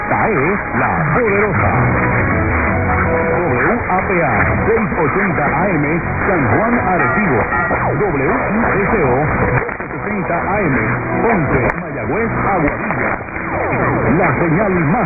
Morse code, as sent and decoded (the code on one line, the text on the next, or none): none